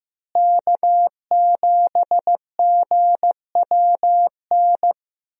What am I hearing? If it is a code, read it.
Morse code, heard as a keyed tone, about 16 wpm, 700 Hz. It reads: K7GWN
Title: USA Call Signs - 5 Characters - Repeated 3x - 15wpm